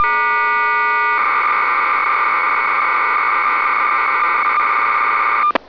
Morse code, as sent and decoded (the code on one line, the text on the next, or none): none